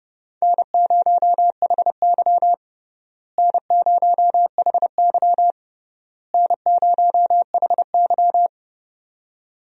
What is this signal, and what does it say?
Morse code, heard as a keyed tone, about 32 wpm, 700 Hz. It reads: N0HY N0HY N0HY
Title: USA Call Signs - 4 Characters - Repeated 3x - 30wpm